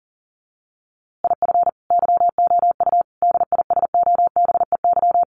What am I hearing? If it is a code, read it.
Morse code, heard as a keyed tone, about 40 wpm, 700 Hz. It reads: IF YOU DISOBEY
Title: A Sound Of Thunder 40wpm